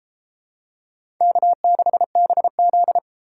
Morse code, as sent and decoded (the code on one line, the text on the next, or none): -.- -.... -... --..
K6BZ